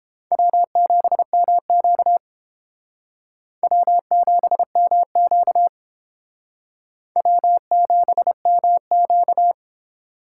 Morse code, heard as a keyed tone, about 31 wpm, 700 Hz. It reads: W7MQ W7MQ W7MQ